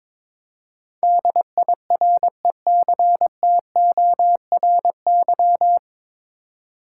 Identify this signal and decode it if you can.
Morse code, heard as a keyed tone, about 22 wpm, 700 Hz. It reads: DIRECTORY
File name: 9 Letter Words - First 3 Letters Given - 22wpm